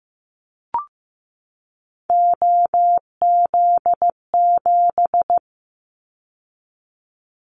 Morse code, heard as a keyed tone, about 15 wpm, 700 Hz. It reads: OZ7
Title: Callsign Prefixes for Top 100 DXCC - 15wpm